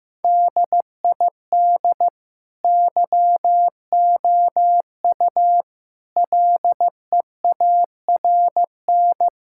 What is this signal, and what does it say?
Morse code, heard as a keyed tone, about 15 wpm, 700 Hz. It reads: DID YOU LEARN
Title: Sentences from Top 1000 Words - Review Part 5 - 15wpm